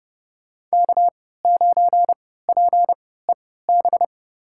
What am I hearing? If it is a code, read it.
Morse code, heard as a keyed tone, about 30 wpm, spaced as for 17 wpm, 700 Hz. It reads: K9PEB
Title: USA Call Signs - 5 Characters - 3x Character Spacing - 30wpm